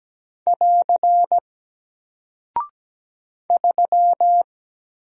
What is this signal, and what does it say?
Morse code, heard as a keyed tone, about 17 wpm, 700 Hz. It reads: <AR> 3
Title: Single Character (Letter:Number:Punctuation:Prosign) Alt 17wpm